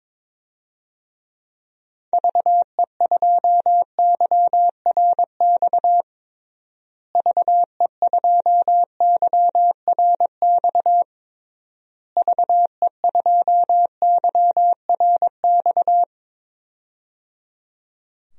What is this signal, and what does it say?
Morse code, heard as a keyed tone, about 21 wpm, 700 Hz. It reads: VE2YRX VE2YRX VE2YRX
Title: DXCC-100 Call Signs - 6 Characters - Repeated 3x - 22wpm